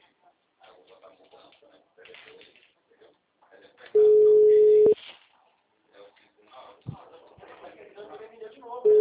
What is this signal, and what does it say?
Ringback tone (Europe, 425 Hz), with speech.